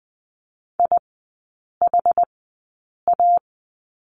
Morse code, heard as a keyed tone, about 20 wpm, spaced as for 8 wpm, 690 Hz. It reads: IHA
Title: 060 Lesson 11e - Letter h - Three Characters 20wpm